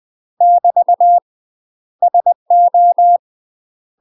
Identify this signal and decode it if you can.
Morse code, heard as a keyed tone, about 20 wpm, 700 Hz. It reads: <BT> SO
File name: UK MASTER QSOs TRAINING / 20WPM / UK MASTER QSOs 0002